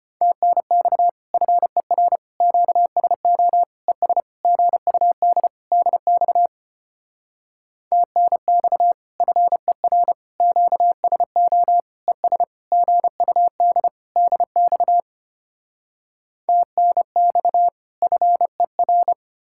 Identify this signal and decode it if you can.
Morse code, heard as a keyed tone, about 33 wpm, 700 Hz. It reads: TNX FER QSO ES GUD DX TNX FER QSO ES GUD DX TNX FER